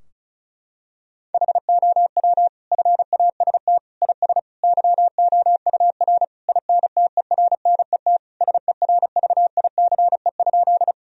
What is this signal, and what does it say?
Morse code, heard as a keyed tone, about 36 wpm, 700 Hz. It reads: HOW FAST IS YOUR INTERNET SERVICE?
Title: Sentences from Top 1000 Words - Review Part 4 - 35wpm